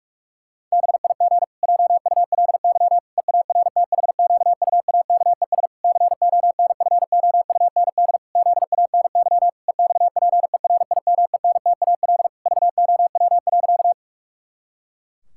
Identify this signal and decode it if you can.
Morse code, heard as a keyed tone, about 46 wpm, 700 Hz. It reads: BIG JULY EARTHQUAKES CONFOUND ZANY EXPERIMENTAL VOW.
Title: Pangram Challenge v2 - 45wpm